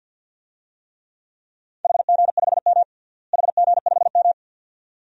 Morse code, heard as a keyed tone, about 52 wpm, 700 Hz. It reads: HG5M HG5M
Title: DXCC-100 Call Signs - 4 Characters - Repeated 2x - 50wpm